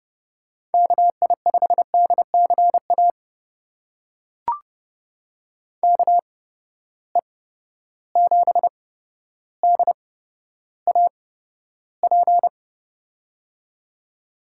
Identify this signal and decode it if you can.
Morse code, heard as a keyed tone, about 31 wpm, spaced as for 13 wpm, 700 Hz. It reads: KI5DCA KE7DAP
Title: USA Call Signs - 6 Characters - 8x Character Spacing - 30wpm